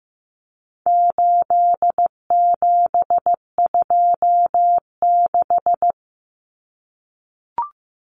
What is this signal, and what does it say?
Morse code, heard as a keyed tone, about 15 wpm, 700 Hz. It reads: 8726
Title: Common Cut Numbers - 4-Digit 15wpm